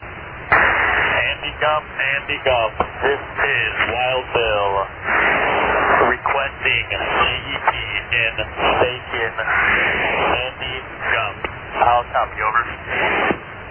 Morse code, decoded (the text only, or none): none